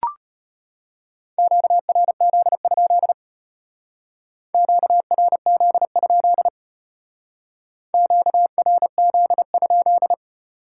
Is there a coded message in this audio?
Yes: Morse code from 1.4 to 10.1 s, heard as a keyed tone, about 33 wpm, 700 Hz. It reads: QRZ? QRZ? QRZ?